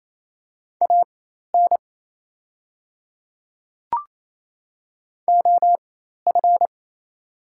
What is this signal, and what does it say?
Morse code, heard as a keyed tone, about 28 wpm, spaced as for 13 wpm, 700 Hz. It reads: AN OF
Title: 4x Character Spacing - Top 100 Words - 28wpm